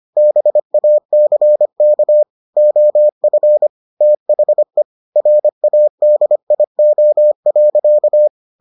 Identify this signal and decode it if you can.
Morse code, heard as a keyed tone, about 25 wpm, 600 Hz. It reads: BACK OF THE RADIO.